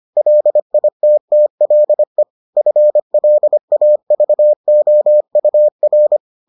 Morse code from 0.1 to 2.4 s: LITTLE

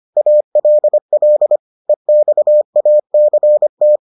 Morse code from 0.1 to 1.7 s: ALL

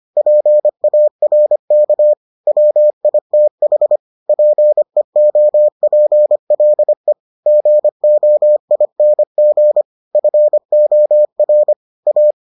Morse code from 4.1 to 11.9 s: PEOPLE GOING FOR

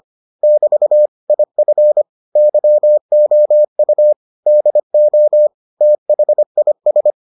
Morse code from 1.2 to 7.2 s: IF YOU DO THIS

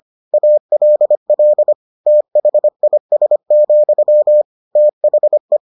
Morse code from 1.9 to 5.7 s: THIS, THE

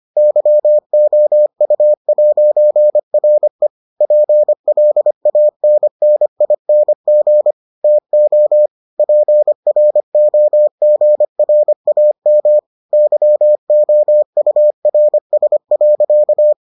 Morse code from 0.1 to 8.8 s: YOU'RE PLANNING TO

